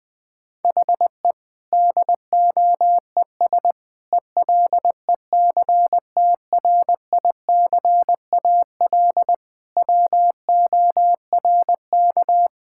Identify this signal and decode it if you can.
Morse code, heard as a keyed tone, about 20 wpm, 700 Hz. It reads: HE DOES ELECTRICAL WORK
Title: Sentences from Top 1000 Words - Review Part 6 - 20wpm